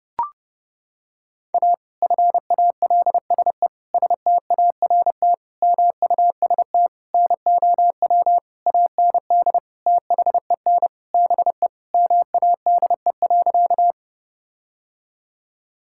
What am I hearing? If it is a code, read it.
Morse code, heard as a keyed tone, about 30 wpm, 700 Hz. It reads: A FALSE START MUST NOW AND THEN BE MADE.
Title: Sentences from Top 1600 Words 30wpm